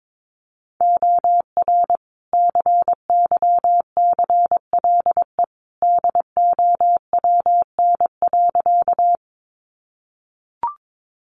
Morse code, heard as a keyed tone, about 22 wpm, 700 Hz. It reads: OR CYCLE DOWN.
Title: Sentences from Top 900 Words 22wpm